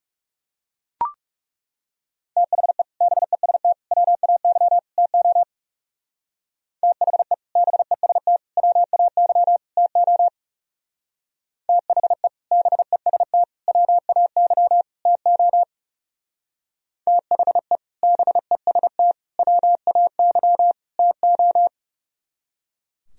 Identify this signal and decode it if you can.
Morse code, heard as a keyed tone, about 37 wpm, 700 Hz. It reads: THE BEST WAY TO THE BEST WAY TO THE BEST WAY TO THE BEST WAY TO
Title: Speed-Racing - Sets of 4 Words 30wpm